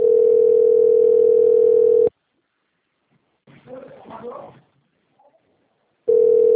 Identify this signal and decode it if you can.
Ringback tone (North America), 440 + 480 Hz.